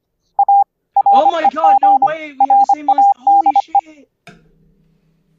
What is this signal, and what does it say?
Morse code, heard as a keyed tone, about 25 wpm, 800 Hz, with speech. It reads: A REG RADE